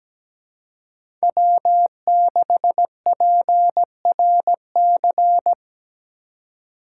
Morse code, heard as a keyed tone, about 17 wpm, 700 Hz. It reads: W6PRC
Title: USA Call Signs - 4x Character Spacing - 17wpm